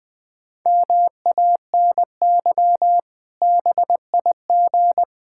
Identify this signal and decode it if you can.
Morse code, heard as a keyed tone, about 20 wpm, 700 Hz. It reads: MANY BIG
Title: Pangram Challenge v2 - 20wpm